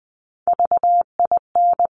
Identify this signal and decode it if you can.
Morse code, heard as a keyed tone, about 20 wpm, 700 Hz. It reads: VIN
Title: Top Three Letter Combinations in Words 20wpm